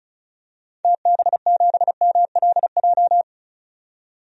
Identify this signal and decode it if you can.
Morse code, heard as a keyed tone, about 34 wpm, 700 Hz. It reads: TB7MLJ
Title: DXCC-100 Call Signs - 6 Characters - 35wpm